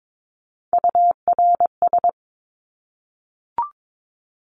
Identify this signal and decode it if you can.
Morse code, heard as a keyed tone, about 22 wpm, 700 Hz. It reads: URS
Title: Abbreviations Repeated 3x - 22wpm